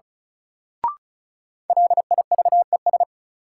Morse code, heard as a keyed tone, about 36 wpm, 700 Hz. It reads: LIVES
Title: Sets of 5 Words Encore - New Word Drill - 35wpm